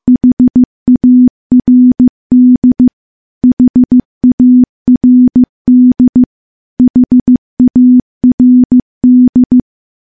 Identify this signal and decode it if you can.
Morse code, heard as a keyed tone, about 15 wpm, 250 Hz. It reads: HARD HARD HARD
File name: morse-training-15WPM